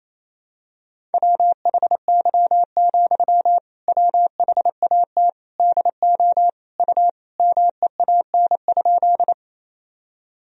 Morse code, heard as a keyed tone, about 27 wpm, 700 Hz. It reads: WHY, WHAT DO U MEAN?
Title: Sentences from Top 200 Words and Abbreviations 28wpm